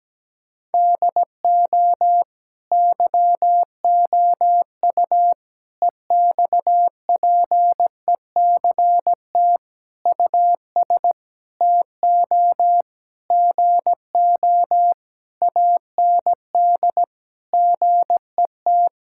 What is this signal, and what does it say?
Morse code, heard as a keyed tone, about 17 wpm, 700 Hz. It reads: DO YOU EXPECT US TO GO AND GET
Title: Sentences from Top 1600 Words 17wpm